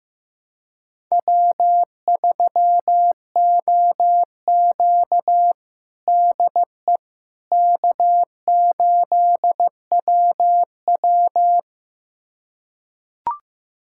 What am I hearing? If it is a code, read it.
Morse code, heard as a keyed tone, about 15 wpm, 700 Hz. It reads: W3OQ DE K8WW
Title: USA Callsign DE Callsign - 4 Characters - 15wpm